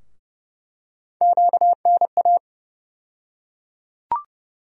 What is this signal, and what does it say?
Morse code, heard as a keyed tone, about 30 wpm, 700 Hz. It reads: QNA
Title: USA Call Sign Postfixes - 3 Letters 30wpm